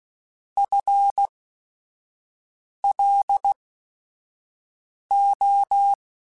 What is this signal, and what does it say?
Morse code, heard as a keyed tone, about 16 wpm, spaced as for 5 wpm, 800 Hz. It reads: FLO
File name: Technical Subjects / Technical Subjects-mp3s / XL @5